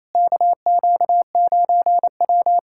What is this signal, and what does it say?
Morse code, heard as a keyed tone, about 28 wpm, 700 Hz. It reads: KQ9W
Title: USA Callsign DE Callsign - 4 Characters - 28wpm